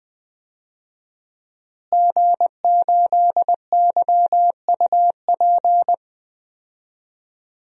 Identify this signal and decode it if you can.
Morse code, heard as a keyed tone, about 21 wpm, 700 Hz. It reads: G8YUP
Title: DXCC-100 Call Signs - 5 Characters - Repeated 2x - 20wpm